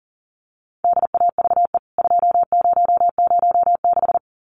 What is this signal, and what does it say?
Morse code, heard as a keyed tone, about 40 wpm, 700 Hz. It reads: DAVE 2006
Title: CWT Exchange 40wpm